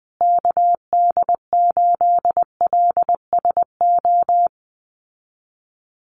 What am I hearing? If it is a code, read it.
Morse code, heard as a keyed tone, about 20 wpm, 700 Hz. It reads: KD8LSO
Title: USA Call Signs Repeated 2x 20wpm